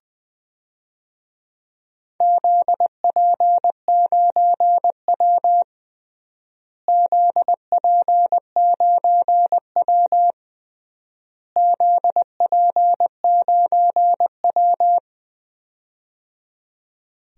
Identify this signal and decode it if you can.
Morse code, heard as a keyed tone, about 21 wpm, 700 Hz. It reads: ZP9W ZP9W ZP9W
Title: DXCC-100 Call Signs - 4 Characters - Repeated 3x - 20wpm